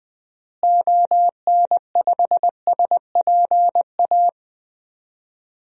Morse code, heard as a keyed tone, about 21 wpm, 700 Hz. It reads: ON5SPA
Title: DXCC-100 Call Signs - 6 Characters - Repeated 2x - 20wpm